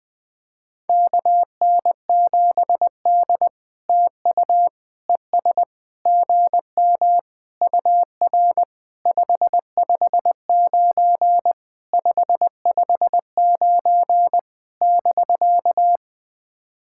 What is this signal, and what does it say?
Morse code, heard as a keyed tone, about 20 wpm, 700 Hz. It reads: KN7D TU ES GM UR 559 559 <BK>
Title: POTA QSO Practice - 20wpm